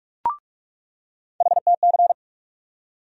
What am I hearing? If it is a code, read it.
Morse code, heard as a keyed tone, about 46 wpm, 700 Hz. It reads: HTC
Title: USA Call Sign Postfixes - 3 Letters 45wpm